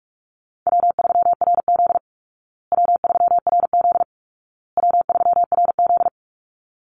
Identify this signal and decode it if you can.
Morse code, heard as a keyed tone, about 44 wpm, 700 Hz. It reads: W3RZ W3RZ W3RZ
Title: Easy Call Signs Repeated 3x - 45wpm